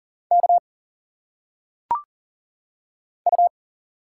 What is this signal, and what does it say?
Morse code, heard as a keyed tone, about 40 wpm, 700 Hz. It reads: K U
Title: Single Letters - Phonetic Pronunciation 40wpm